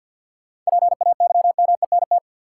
Morse code, heard as a keyed tone, about 50 wpm, 700 Hz. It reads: PAYMENT
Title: Sentences from Top 1200 Words - New Word Drill Alt - 50wpm